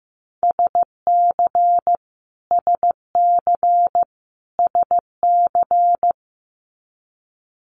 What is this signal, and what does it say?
Morse code, heard as a keyed tone, about 15 wpm, 700 Hz. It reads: SC SC SC
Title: Multi-letter Phonemes Repeated 3x 15wpm